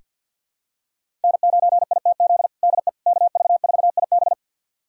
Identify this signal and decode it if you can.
Morse code, heard as a keyed tone, about 46 wpm, 700 Hz. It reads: N9ITZ DE KV4IB
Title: USA Callsign DE Callsign - 5 Characters - 50wpm